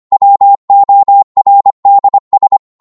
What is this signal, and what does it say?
Morse code, heard as a keyed tone, about 25 wpm, 800 Hz. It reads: WORDS